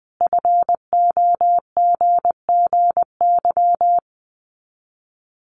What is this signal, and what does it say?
Morse code, heard as a keyed tone, about 20 wpm, 700 Hz. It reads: FOGGY
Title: Common QSO Phrases 20wpm